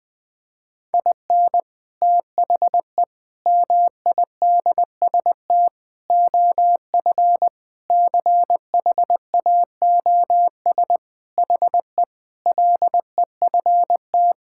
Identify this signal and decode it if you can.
Morse code, heard as a keyed tone, about 20 wpm, 700 Hz. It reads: IN THE MIDST OF CHAOS HE LEFT
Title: Sentences with 4-Word Phrases - 20wpm